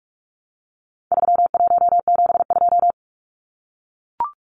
Morse code, heard as a keyed tone, about 44 wpm, 700 Hz. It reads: 3172
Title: Numbers - 4-Digit 45wpm